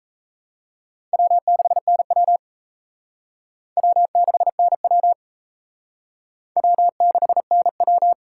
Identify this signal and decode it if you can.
Morse code, heard as a keyed tone, about 36 wpm, 700 Hz. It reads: W6NW W6NW W6NW